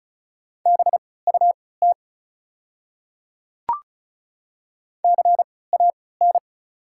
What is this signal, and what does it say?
Morse code, heard as a keyed tone, about 36 wpm, spaced as for 20 wpm, 690 Hz. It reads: BUT CAN
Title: 3x Character Spacing - Top 100 Words - 35wpm